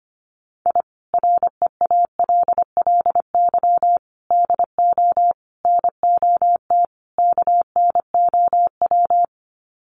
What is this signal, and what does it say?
Morse code, heard as a keyed tone, about 25 wpm, 700 Hz. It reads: I REALLY DO NOT KNOW
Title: Sets of 5 Words Encore 25wpm